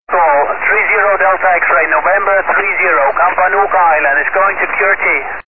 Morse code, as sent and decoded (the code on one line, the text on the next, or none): none